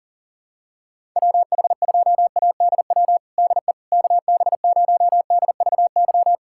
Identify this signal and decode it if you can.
Morse code, heard as a keyed tone, about 40 wpm, 700 Hz. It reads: WH2ADW DE KB0DVY
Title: USA Callsign DE Callsign - 6 Characters - 40wpm